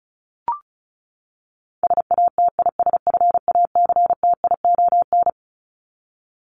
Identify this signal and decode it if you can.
Morse code, heard as a keyed tone, about 34 wpm, 700 Hz. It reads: SATISFACTION
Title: Top 1751-2000 Words 35wpm